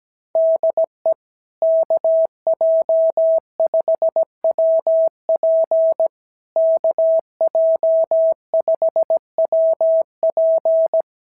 Morse code, heard as a keyed tone, about 17 wpm, 660 Hz. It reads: DE KJ5WP KJ5WP
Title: SOTA QSO Practice 17wpm